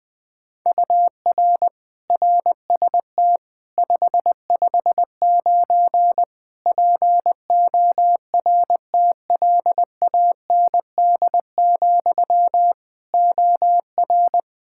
Morse code, heard as a keyed tone, about 20 wpm, 700 Hz. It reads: UR RST 559 PORTLAND, OR
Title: K1USN SST QSO Practice 20wpm